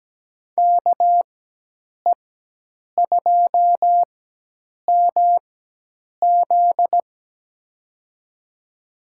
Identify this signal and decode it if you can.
Morse code, heard as a keyed tone, about 17 wpm, spaced as for 8 wpm, 700 Hz. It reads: KE2MZ
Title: USA Call Signs - 4x Character Spacing - 17wpm